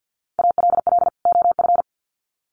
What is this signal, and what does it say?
Morse code, heard as a keyed tone, about 50 wpm, 700 Hz. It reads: ALL OF
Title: Sets of 2 Words 50wpm